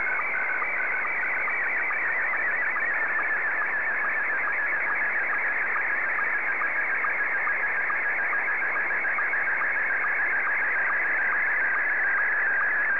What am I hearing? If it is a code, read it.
SSTV (slow-scan television), Scottie 1.
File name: sstv / S1-ZL2TJM